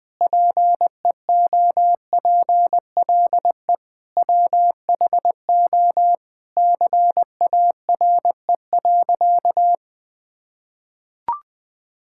Morse code, heard as a keyed tone, about 21 wpm, 700 Hz. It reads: PEOPLE WHO CARE.
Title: Sentences with 5-Word Phrases - 20wpm